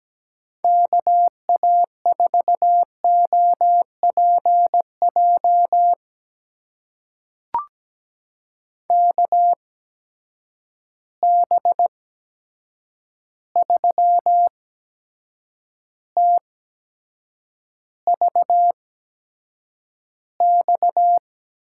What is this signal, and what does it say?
Morse code, heard as a keyed tone, about 17 wpm, spaced as for 7 wpm, 700 Hz. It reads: KA4OPJ KB3TVX
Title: USA Call Signs - 6 Characters - 8x Character Spacing - 17wpm